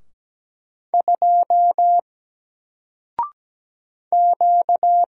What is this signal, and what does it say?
Morse code, heard as a keyed tone, about 17 wpm, 700 Hz. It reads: 2 Q
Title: Single Letter-Number 17wpm